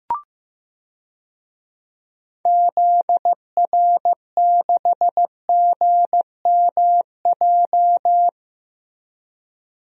Morse code, heard as a keyed tone, about 15 wpm, 700 Hz. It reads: ZR6GMJ
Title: DXCC-100 Call Signs - 6 Characters - Repeated 3x - 15wpm